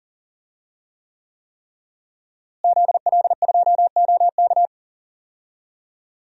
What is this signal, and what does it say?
Morse code, heard as a keyed tone, about 40 wpm, 700 Hz. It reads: ZL2OK
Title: DXCC-100 Call Signs - 5 Characters - Repeated 3x - 40wpm